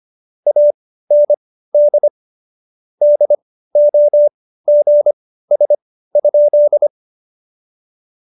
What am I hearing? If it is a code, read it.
Morse code, heard as a keyed tone, about 25 wpm, spaced as for 15 wpm, 590 Hz. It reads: AND DOGS?
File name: Jokes-25-char-15-eff